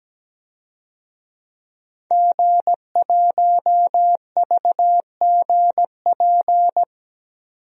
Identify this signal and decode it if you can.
Morse code, heard as a keyed tone, about 17 wpm, 700 Hz. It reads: G1VGP
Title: DXCC-100 Call Signs - 5 Characters - 17wpm